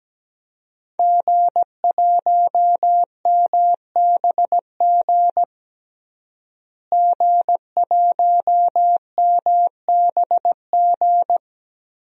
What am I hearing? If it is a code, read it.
Morse code, heard as a keyed tone, about 17 wpm, 700 Hz. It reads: G1MBG G1MBG
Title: DXCC-100 Call Signs - 5 Characters - Repeated 3x - 17wpm